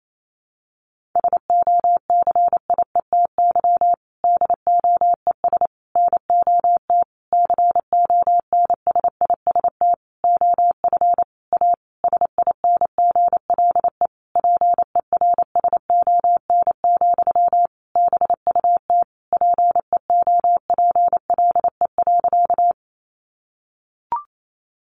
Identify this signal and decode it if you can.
Morse code, heard as a keyed tone, about 28 wpm, 700 Hz. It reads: SOCIETY DOES NOT CONSIST OF A SINGLE PERSON, BUT PEOPLE.
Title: Sentences from Top 1400 Words 28wpm